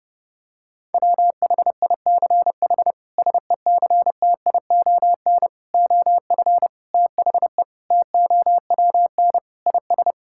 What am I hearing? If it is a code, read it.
Morse code, heard as a keyed tone, about 30 wpm, 700 Hz. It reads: WHICH SECTION OF THE TOWN IS
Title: Sentences from Top 1000 Words - Review Part 5 - 30wpm